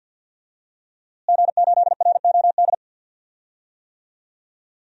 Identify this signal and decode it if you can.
Morse code, heard as a keyed tone, about 52 wpm, 700 Hz. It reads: G8ROD
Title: DXCC-100 Call Signs - 5 Characters - Repeated 3x - 50wpm